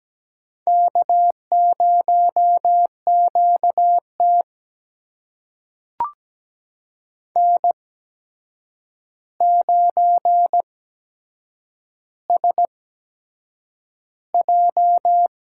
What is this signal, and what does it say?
Morse code, heard as a keyed tone, about 17 wpm, spaced as for 7 wpm, 700 Hz. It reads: K0QT N9SJ
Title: USA Call Signs - 4 Characters - 8x Character Spacing - 17wpm